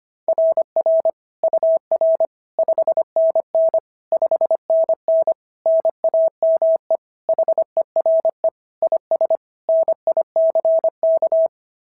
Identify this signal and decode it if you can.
Morse code, heard as a keyed tone, about 24 wpm, 650 Hz. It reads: RR UR 5NN 5NN NAME HERE IS NICK
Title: SOTA QSO Practice 25wpm